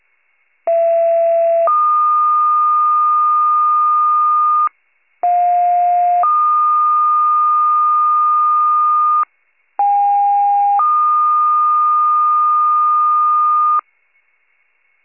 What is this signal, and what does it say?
Two-tone page, heard as a one-second tone, then a three-second tone, about 670 then 1220 Hz, about 710 then 1220 Hz, about 810 then 1220 Hz. It is heard from 0.7 to 13.8 s.